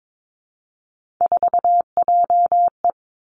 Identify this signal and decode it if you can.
Morse code, heard as a keyed tone, about 22 wpm, 700 Hz. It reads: 4JE
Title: Three Challenging Letter-Number Combinations 22wpm